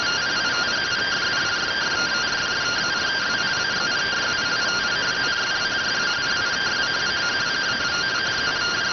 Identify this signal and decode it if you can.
Two-tone teleprinter signal, about 170 Hz shift.